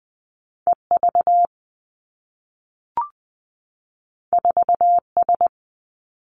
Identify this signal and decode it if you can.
Morse code, heard as a keyed tone, about 20 wpm, 700 Hz. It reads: E V 4 S
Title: Two Challenging Letter-Number Combinations 20wpm